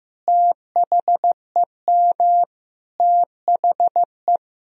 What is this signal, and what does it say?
Morse code, heard as a keyed tone, about 16 wpm, 700 Hz. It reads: THEM THE
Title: Sentences from Top 1000 Words - Review Part 2 - 15wpm